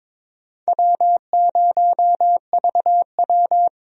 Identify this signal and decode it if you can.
Morse code, heard as a keyed tone, about 22 wpm, 700 Hz. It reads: W0VW
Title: USA Call Signs - 4 Characters - 22wpm